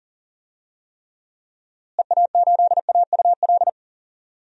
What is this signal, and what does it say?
Morse code, heard as a keyed tone, about 40 wpm, 700 Hz. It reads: EA8AUL
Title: DXCC-100 Call Signs - 6 Characters - Repeated 3x - 40wpm